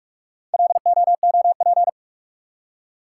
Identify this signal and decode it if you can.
Morse code, heard as a keyed tone, about 43 wpm, 700 Hz. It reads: LOOP